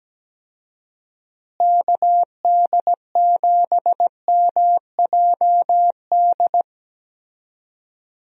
Morse code, heard as a keyed tone, about 17 wpm, 700 Hz. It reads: KD7MJD
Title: USA Call Signs - 6 Characters - Repeated 3x - 17wpm